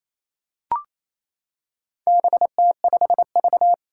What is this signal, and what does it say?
Morse code, heard as a keyed tone, about 28 wpm, 700 Hz. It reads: BT5V
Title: Four Challenging Letter-Number Combinations 28wpm